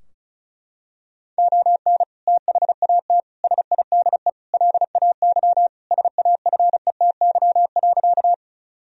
Morse code, heard as a keyed tone, about 34 wpm, 700 Hz. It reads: ON THAT SIDE LAY SAFETY.
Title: Sentences from Top 1000 Words - Review Part 4 - 35wpm